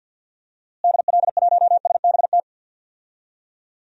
Morse code, heard as a keyed tone, about 52 wpm, 700 Hz. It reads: DL1SBT
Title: DXCC-100 Call Signs - 6 Characters - Repeated 3x - 50wpm